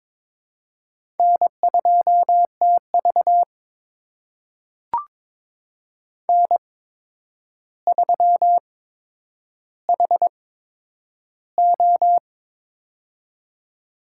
Morse code, heard as a keyed tone, about 22 wpm, spaced as for 9 wpm, 700 Hz. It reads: N2TV N3HO
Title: USA Call Signs - 4 Characters - 8x Character Spacing - 22wpm